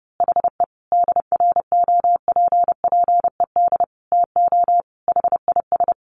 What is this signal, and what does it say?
Morse code, heard as a keyed tone, about 30 wpm, 700 Hz. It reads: HE DROPPED TO HIS
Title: A Sound Of Thunder 30wpm